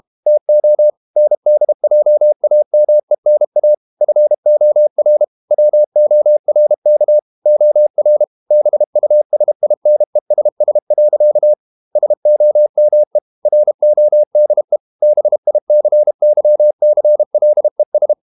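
Morse code, heard as a keyed tone, about 32 wpm, 600 Hz. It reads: TO NDJAMENA FOR WORK OR BUSINESS. SOME RODE BICYCLES
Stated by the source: Morse code practice at 32 wpm from CW med Gnister, reading Ken Follett's novel Never in English.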